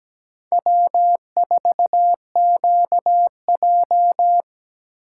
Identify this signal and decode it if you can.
Morse code, heard as a keyed tone, about 17 wpm, 700 Hz. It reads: W4QJ
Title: USA Call Signs - 2x Character Spacing - 17wpm